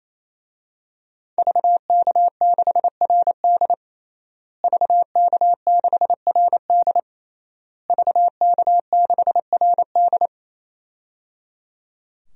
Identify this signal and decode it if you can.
Morse code, heard as a keyed tone, about 28 wpm, 700 Hz. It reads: VK6RD VK6RD VK6RD
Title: DXCC-100 Call Signs - 5 Characters - Repeated 3x - 28wpm